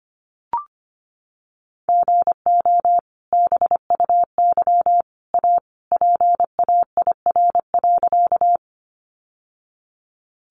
Morse code, heard as a keyed tone, about 25 wpm, 700 Hz. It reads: GO BUY A PAIR.
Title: Sentences from Top 1200 Words 25wpm